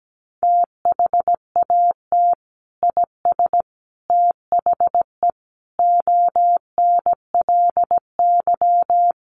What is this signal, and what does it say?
Morse code, heard as a keyed tone, about 17 wpm, 700 Hz. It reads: THAT IS THE ONLY
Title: Sentences from Top 800 Words 17wpm